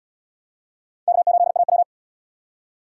Morse code, heard as a keyed tone, about 77 wpm, 700 Hz. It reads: K8AW